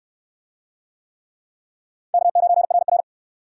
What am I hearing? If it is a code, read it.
Morse code, heard as a keyed tone, about 69 wpm, 700 Hz. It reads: K9RR